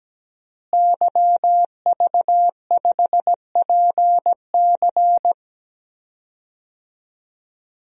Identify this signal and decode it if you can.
Morse code, heard as a keyed tone, about 17 wpm, 700 Hz. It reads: YV5PC
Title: DXCC-100 Call Signs - 5 Characters - Repeated 3x - 17wpm